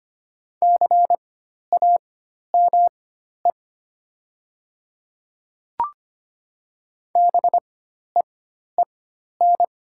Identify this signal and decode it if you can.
Morse code, heard as a keyed tone, about 25 wpm, spaced as for 12 wpm, 700 Hz. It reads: CAME BEEN
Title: 4x Character Spacing - Top 200 Words - 25wpm